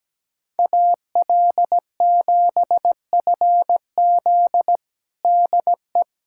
Morse code, heard as a keyed tone, about 17 wpm, 700 Hz. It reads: AL7FZ DE
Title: USA Callsign DE Callsign - 5 Characters - 17wpm